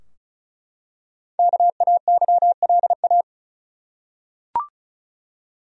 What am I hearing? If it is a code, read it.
Morse code, heard as a keyed tone, about 34 wpm, 700 Hz. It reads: KAYLA
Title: USA Common Names - 35wpm